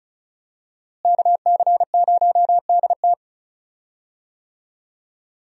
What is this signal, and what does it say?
Morse code, heard as a keyed tone, about 36 wpm, 700 Hz. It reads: KC0DT
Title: USA Call Signs - 5 Characters - Repeated 3x - 35wpm